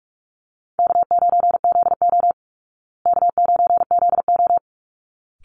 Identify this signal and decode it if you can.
Morse code, heard as a keyed tone, about 44 wpm, 700 Hz. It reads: K9ZO K9ZO
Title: Easy Call Signs Repeated 2x - 45wpm